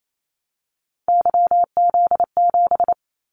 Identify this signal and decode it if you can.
Morse code, heard as a keyed tone, about 28 wpm, 700 Hz. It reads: YZ7
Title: Callsign Prefixes for Top 25 DXCC - 28wpm